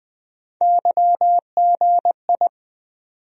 Morse code, heard as a keyed tone, about 20 wpm, 700 Hz. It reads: YGI
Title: USA Call Sign Postfixes - 3 Letters 20wpm